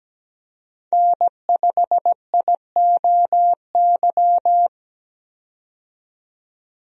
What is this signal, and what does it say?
Morse code, heard as a keyed tone, about 17 wpm, 700 Hz. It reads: N5IOY
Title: USA Call Signs - 5 Characters - 17wpm